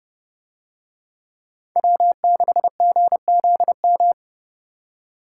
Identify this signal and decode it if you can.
Morse code, heard as a keyed tone, about 29 wpm, 700 Hz. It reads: W6GZM